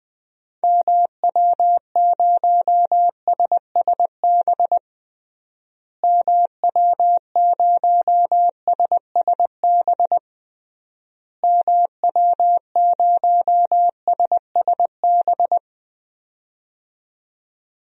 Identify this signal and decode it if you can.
Morse code, heard as a keyed tone, about 21 wpm, 700 Hz. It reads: MW0SSB MW0SSB MW0SSB
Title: DXCC-100 Call Signs - 6 Characters - Repeated 3x - 20wpm